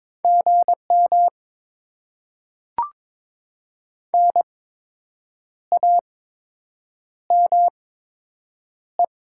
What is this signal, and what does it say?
Morse code, heard as a keyed tone, about 22 wpm, spaced as for 7 wpm, 700 Hz. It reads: GM NAME
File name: Top 20 QSO Elements - 8x Character Spacing - 22wpm